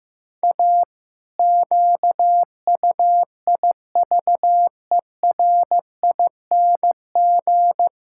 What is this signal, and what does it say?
Morse code, heard as a keyed tone, about 15 wpm, 700 Hz. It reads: A QUIVERING
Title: Pangram Challenge v2 - 15wpm